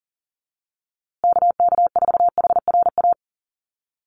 Morse code, heard as a keyed tone, about 40 wpm, 700 Hz. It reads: KK4HRA
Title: USA Call Signs Repeated 2x 40wpm